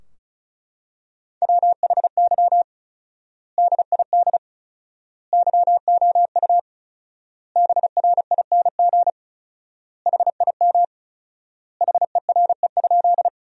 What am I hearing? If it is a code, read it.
Morse code, heard as a keyed tone, about 36 wpm, 700 Hz. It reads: WHY DID YOU BRING HIM HERE?
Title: 4x Word-Spacing - Sentences from Top 500 Words - 35wpm